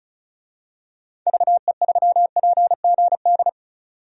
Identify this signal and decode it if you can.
Morse code, heard as a keyed tone, about 36 wpm, 700 Hz. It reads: VE3PGD
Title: DXCC-100 Call Signs - 6 Characters - 35wpm